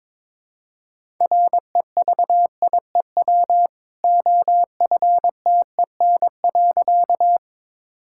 Morse code, heard as a keyed tone, about 23 wpm, 700 Hz. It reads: REVIEW OFTEN.
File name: practice / 22wpm / Easy Sentences with Hints - 22wpm